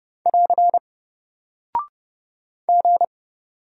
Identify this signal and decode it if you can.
Morse code, heard as a keyed tone, about 30 wpm, 700 Hz. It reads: <AR> G
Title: Single Character (Letter:Number:Punctuation:Prosign) Alt 30wpm